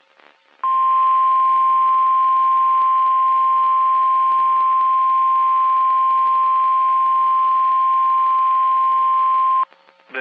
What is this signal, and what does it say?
Emergency Alert System (SAME) tones (the 1050 Hz attention tone). It is heard from 0.6 to 9.7 s.
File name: eas messages / EAS WXM31 Elmira